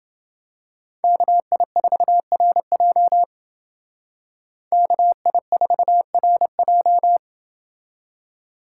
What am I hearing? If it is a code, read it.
Morse code, heard as a keyed tone, about 29 wpm, 700 Hz. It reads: KI4RJ KI4RJ